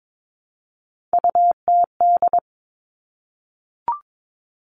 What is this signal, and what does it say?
Morse code, heard as a keyed tone, about 22 wpm, 700 Hz. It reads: UTD
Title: Three Challenging Letter-Number Combinations 22wpm